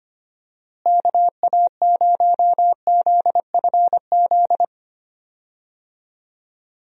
Morse code, heard as a keyed tone, about 25 wpm, 700 Hz. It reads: KA0ZFZ
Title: USA Call Signs - 6 Characters - Repeated 2x - 25wpm